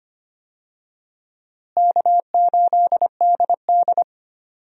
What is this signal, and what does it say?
Morse code, heard as a keyed tone, about 25 wpm, 700 Hz. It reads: K8DD
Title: USA Call Signs - 4 Characters - Repeated 3x - 25wpm